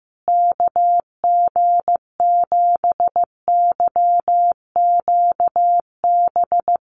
Morse code, heard as a keyed tone, about 15 wpm, 700 Hz. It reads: KG7YQB
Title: USA Call Signs Repeated 2x 15wpm